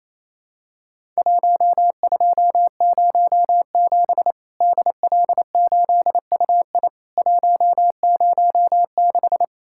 Morse code, heard as a keyed tone, about 28 wpm, 700 Hz. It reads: 1207 DL8UI 106
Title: Worked All Europe DX Contest - QTC Exchange - 28wpm